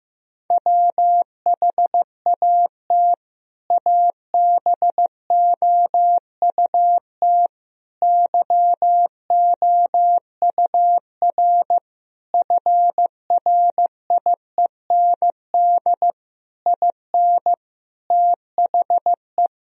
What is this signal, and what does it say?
Morse code, heard as a keyed tone, about 15 wpm, 700 Hz. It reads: WHAT ABOUT YOUR FRIEND IN THE
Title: Sentences from Top 1000 Words - Review Part 1 - 15wpm